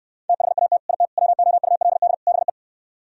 Morse code, heard as a keyed tone, about 76 wpm, 700 Hz. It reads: THAT IT WOULD BE